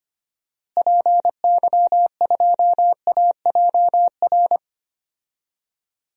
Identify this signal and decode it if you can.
Morse code, heard as a keyed tone, about 25 wpm, 700 Hz. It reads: PY2AJR
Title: DXCC-100 Call Signs - 6 Characters - Repeated 3x - 25wpm